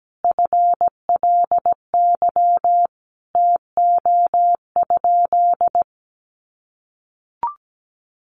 Morse code, heard as a keyed tone, about 17 wpm, 700 Hz. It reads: FLY TO?
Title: Sentences from Top 500 Words 17wpm